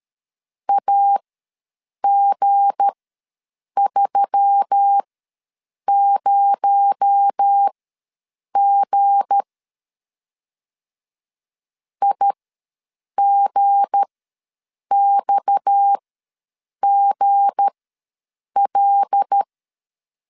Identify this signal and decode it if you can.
Morse code, heard as a keyed tone, about 13 wpm, spaced as for 7 wpm, 800 Hz. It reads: AG30G IGXGL
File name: LEZIONI / 7 GF9Z / 7 G ws 12wpm 3140